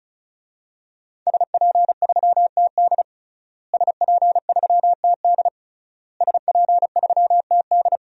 Morse code, heard as a keyed tone, about 34 wpm, 700 Hz. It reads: SP3TD SP3TD SP3TD
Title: DXCC-100 Call Signs - 5 Characters - Repeated 3x - 35wpm